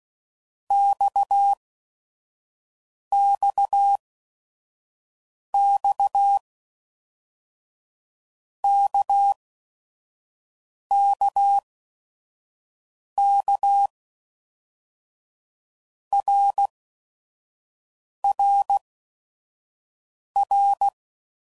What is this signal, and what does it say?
Morse code, heard as a keyed tone, about 16 wpm, spaced as for 5 wpm, 800 Hz. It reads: XXX KKK RRR
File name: Basic Letter Groups By7Groups / RKPX / RKPX@5 f15